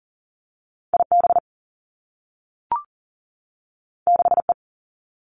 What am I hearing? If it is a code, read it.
Morse code, heard as a keyed tone, about 42 wpm, 700 Hz. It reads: IB 6E
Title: Two Challenging Letter-Number Combinations 40wpm